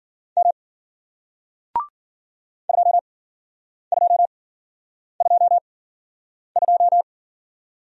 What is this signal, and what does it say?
Morse code, heard as a keyed tone, about 52 wpm, spaced as for 9 wpm, 700 Hz. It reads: M 2222